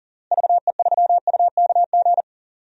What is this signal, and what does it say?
Morse code, heard as a keyed tone, about 40 wpm, 700 Hz. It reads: VE3UKG
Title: DXCC-100 Call Signs - 6 Characters - Repeated 3x - 40wpm